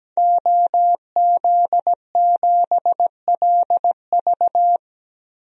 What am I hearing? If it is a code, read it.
Morse code, heard as a keyed tone, about 17 wpm, 700 Hz. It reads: OZ7LV
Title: DXCC-100 Call Signs - 5 Characters - 17wpm